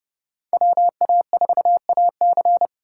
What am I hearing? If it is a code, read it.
Morse code, heard as a keyed tone, about 28 wpm, 700 Hz. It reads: WA4AC